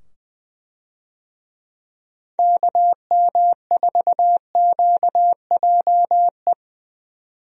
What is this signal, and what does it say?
Morse code, heard as a keyed tone, about 20 wpm, 700 Hz. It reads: KM4QJE
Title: USA Call Signs - 6 Characters - 20wpm